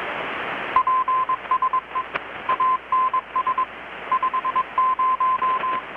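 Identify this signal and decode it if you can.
Morse code, heard as a keyed tone, about 22 wpm, 1030 Hz, with radio noise. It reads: PSE ANS 50
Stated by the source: Morse from KPH's Night of Nights XX maritime radiotelegraphy event, received in Chiapas, Mexico about 3,800 km away on a Kenwood R-600 with a grounded 10 m long-wire antenna.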